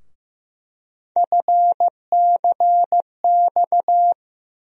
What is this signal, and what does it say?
Morse code, heard as a keyed tone, about 16 wpm, 700 Hz. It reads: FCX
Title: USA Call Sign Postfixes - 3 Letters 15wpm